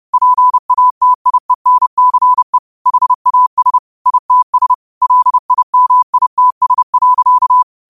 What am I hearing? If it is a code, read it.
Morse code, heard as a keyed tone, about 30 wpm, 1000 Hz. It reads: PATIENCE HAS ITS LIMITS.